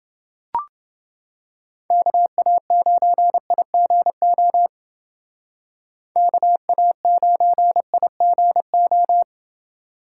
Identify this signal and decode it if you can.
Morse code, heard as a keyed tone, about 28 wpm, 700 Hz. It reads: KA9IGO KA9IGO